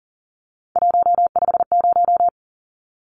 Morse code, heard as a keyed tone, about 40 wpm, 700 Hz. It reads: 150
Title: Numbers - 3-Digit 40wpm